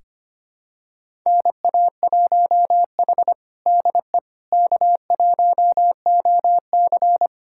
Morse code, heard as a keyed tone, about 25 wpm, 700 Hz. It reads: NA1H DE K1OC
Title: USA Callsign DE Callsign - 4 Characters - 25wpm